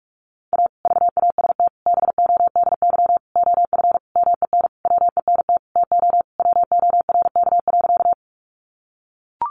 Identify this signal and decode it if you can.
Morse code, heard as a keyed tone, about 44 wpm, 700 Hz. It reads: A VAST BODY OF MEN WENT TO WORK.
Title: Sentences from Top 1300 Words 45wpm